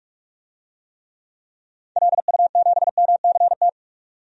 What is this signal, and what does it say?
Morse code, heard as a keyed tone, about 45 wpm, 700 Hz. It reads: LU7MCT
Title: DXCC-100 Call Signs - 6 Characters - 45wpm